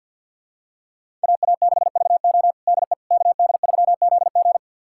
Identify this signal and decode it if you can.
Morse code, heard as a keyed tone, about 52 wpm, 690 Hz. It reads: AA6VO DE KD3ZG
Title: USA Callsign DE Callsign - 5 Characters - 50wpm